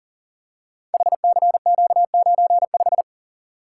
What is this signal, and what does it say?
Morse code, heard as a keyed tone, about 39 wpm, 690 Hz. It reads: HCQ95
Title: Code Groups - 5 Letter:Numbers - 40wpm